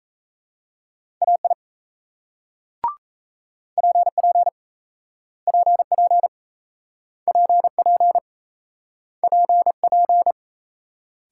Morse code, heard as a keyed tone, about 33 wpm, 700 Hz. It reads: AI PP PP PP PP
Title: Speed-Racing Multi-letter Phonemes - 28wpm